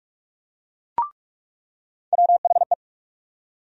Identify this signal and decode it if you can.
Morse code, heard as a keyed tone, about 46 wpm, 700 Hz. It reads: WHE
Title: USA Call Sign Postfixes - 3 Letters 45wpm